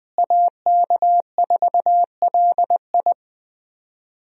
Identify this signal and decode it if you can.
Morse code, heard as a keyed tone, about 21 wpm, 700 Hz. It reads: AK4LI
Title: USA Call Signs - 5 Characters - Repeated 3x - 20wpm